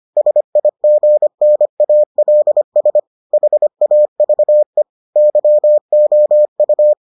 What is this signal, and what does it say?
Morse code, heard as a keyed tone, about 25 wpm, 600 Hz. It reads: SIGNALS HAVE YOU